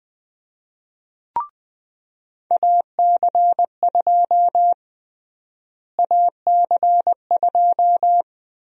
Morse code, heard as a keyed tone, about 20 wpm, 700 Hz. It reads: AC2 AC2
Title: USA Call Sign Prefixes - Repeated 2x - 20wpm